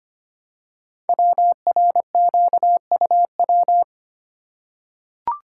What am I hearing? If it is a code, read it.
Morse code, heard as a keyed tone, about 25 wpm, 700 Hz. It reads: WRQUW
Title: Code Groups - 5 Letters - 25wpm